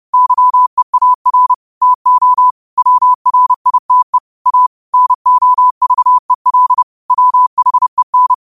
Morse code, heard as a keyed tone, about 30 wpm, 1000 Hz. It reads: YEAR TO WRITE A NOVEL WHEN
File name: qotdfiles / 30wpm